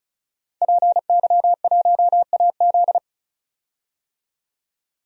Morse code, heard as a keyed tone, about 36 wpm, 700 Hz. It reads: PY1AZ
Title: DXCC-100 Call Signs - 5 Characters - 35wpm